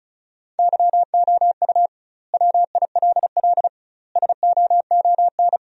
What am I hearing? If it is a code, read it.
Morse code, heard as a keyed tone, about 34 wpm, 700 Hz. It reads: YOU WILL SOON
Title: 2x Word-Spacing - Sentences from Top 400 Words - 35wpm